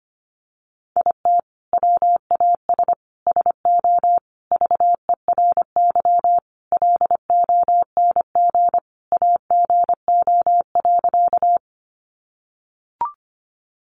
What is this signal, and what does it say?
Morse code, heard as a keyed tone, about 25 wpm, 700 Hz. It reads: IT WAS SO VERY LONG AGO.
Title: Sentences from Top 500 Words 25wpm